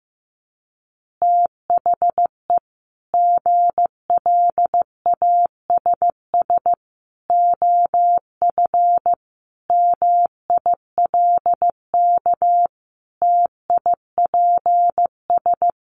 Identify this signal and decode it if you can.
Morse code, heard as a keyed tone, about 15 wpm, 700 Hz. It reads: THE GLASS OF MILK TIPS
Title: Additional Easy Phrases 15wpm